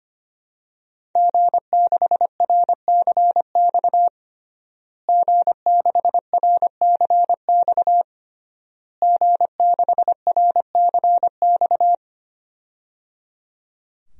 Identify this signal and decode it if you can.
Morse code, heard as a keyed tone, about 25 wpm, 700 Hz. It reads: G6RCX G6RCX G6RCX
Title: DXCC-100 Call Signs - 5 Characters - Repeated 3x - 25wpm